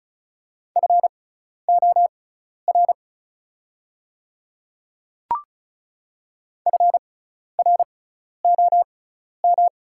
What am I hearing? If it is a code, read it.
Morse code, heard as a keyed tone, about 36 wpm, spaced as for 12 wpm, 700 Hz. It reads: FOR FROM
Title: 6x Character Spacing - Top 100 Words - 35wpm